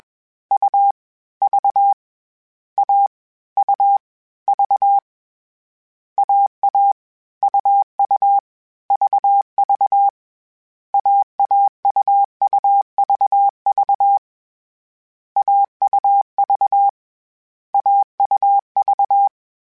Morse code, heard as a keyed tone, about 21 wpm, 800 Hz. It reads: U V A U V AA UU VV AAUUVV AUV AUV
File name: Short Runs / Short Runs-mp3s / AUV AUV4 @20